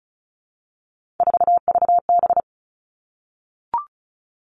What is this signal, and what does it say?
Morse code, heard as a keyed tone, about 36 wpm, 700 Hz. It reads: 4VB
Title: Three Challenging Letter-Number Combinations 35wpm